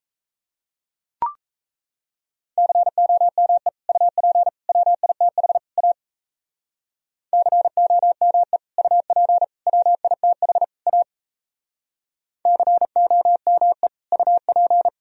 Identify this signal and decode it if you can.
Morse code, heard as a keyed tone, about 34 wpm, 700 Hz. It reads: COME UP WITH A COME UP WITH A COME UP
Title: Speed-Racing - Sets of 4 Words 28wpm